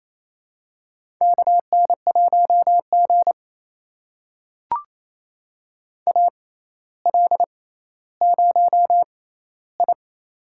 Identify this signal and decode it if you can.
Morse code, heard as a keyed tone, about 28 wpm, spaced as for 14 wpm, 700 Hz. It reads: KN1G AL0I
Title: USA Call Signs - 4 Characters - 6x Character Spacing - 28wpm